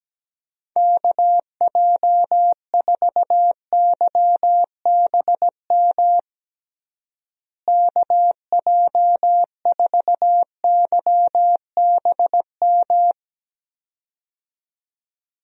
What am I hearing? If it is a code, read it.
Morse code, heard as a keyed tone, about 17 wpm, 700 Hz. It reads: KJ4YBM KJ4YBM
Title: USA Call Signs - 6 Characters - Repeated 2x - 17wpm